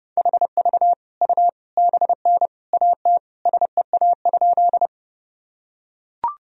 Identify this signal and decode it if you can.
Morse code, heard as a keyed tone, about 32 wpm, 700 Hz. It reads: HV U BN AT SEA?
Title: Sentences from Top 300 Words and Abbreviations 30wpm